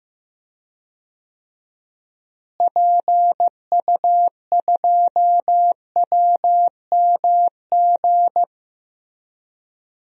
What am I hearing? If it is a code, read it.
Morse code, heard as a keyed tone, about 15 wpm, 700 Hz. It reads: PU2WMG
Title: DXCC-100 Call Signs - 6 Characters - Repeated 2x - 15wpm